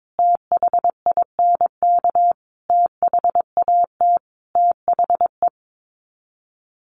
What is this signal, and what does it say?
Morse code, heard as a keyed tone, about 22 wpm, 700 Hz. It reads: THINK THAT THE
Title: Sets of 5 Words Encore 22wpm